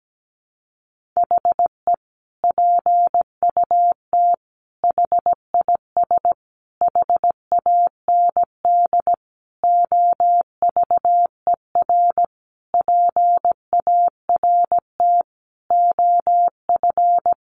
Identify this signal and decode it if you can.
Morse code, heard as a keyed tone, about 17 wpm, 700 Hz. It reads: HE PUT HIS HAND OVER PART OF
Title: Sentences from Top 400 Words 17wpm